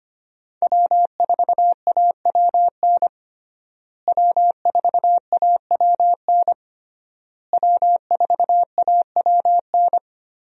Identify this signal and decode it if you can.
Morse code, heard as a keyed tone, about 25 wpm, 700 Hz. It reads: W4AWN W4AWN W4AWN
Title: USA Call Signs - 5 Characters - Repeated 3x - 25wpm